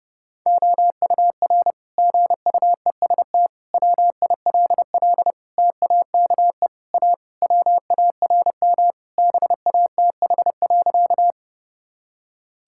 Morse code, heard as a keyed tone, about 28 wpm, 700 Hz. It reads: OUR GUEST WILL TAKE A WARM BATH.
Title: Sentences from Top 1600 Words 30wpm